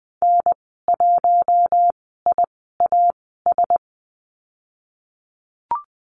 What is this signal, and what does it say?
Morse code, heard as a keyed tone, about 20 wpm, spaced as for 15 wpm, 700 Hz. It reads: N1IAS
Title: 062 Lesson 11g - Letter h - Call Signs 20wpm